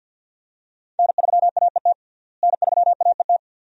Morse code, heard as a keyed tone, about 52 wpm, 700 Hz. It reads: N3AET N3AET
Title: USA Call Signs - 5 Characters - Repeated 2x - 50wpm